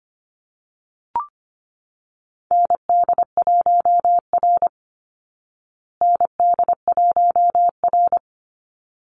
Morse code, heard as a keyed tone, about 25 wpm, 700 Hz. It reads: ND1R ND1R
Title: Call Signs Repeated 3x - 25wpm